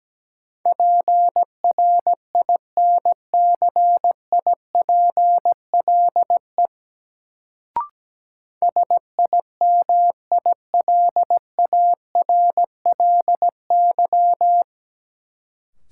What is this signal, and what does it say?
Morse code, heard as a keyed tone, about 17 wpm, 700 Hz. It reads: PRINCIPLE SIMILARLY
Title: Sentences from Top 1400 Words - New Word Drill - 17wpm